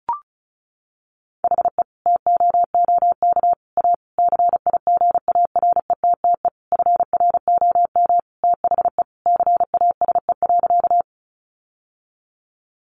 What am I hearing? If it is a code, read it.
Morse code, heard as a keyed tone, about 34 wpm, 700 Hz. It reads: HE TOOK A CIGARETTE FROM THE CASE.
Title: Sentences from Top 1000 Words 35wpm